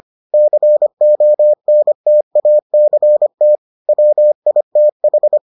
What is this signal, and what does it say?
Morse code, heard as a keyed tone, about 25 wpm, 600 Hz. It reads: CONTACT WITH